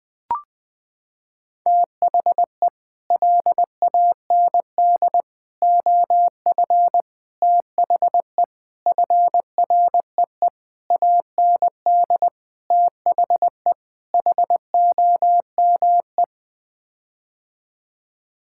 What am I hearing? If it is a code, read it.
Morse code, heard as a keyed tone, about 20 wpm, 700 Hz. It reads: THE LAND OF THE FREE AND THE HOME
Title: Sets of 8 Words - 20wpm